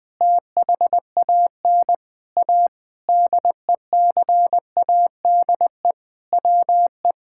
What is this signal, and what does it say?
Morse code, heard as a keyed tone, about 20 wpm, 700 Hz. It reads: THAN A DECADE WE
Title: Sentences with 9-Word Phrases - 20wpm